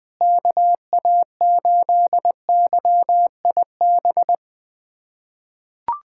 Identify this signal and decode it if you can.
Morse code, heard as a keyed tone, about 21 wpm, 700 Hz. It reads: KA8YIB
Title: USA Callsign DE Callsign 20wpm